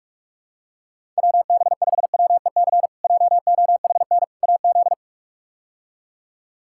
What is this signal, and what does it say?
Morse code, heard as a keyed tone, about 41 wpm, 700 Hz. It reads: WB5WEC JOHN AZ
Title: K1USN SST QSO Practice - 45wpm